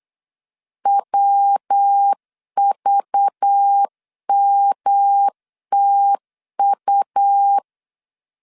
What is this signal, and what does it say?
Morse code, heard as a keyed tone, about 8 wpm, 800 Hz. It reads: WVMTU
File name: LEZIONI / 8 YJ45 / 8 Y 8wpm 3245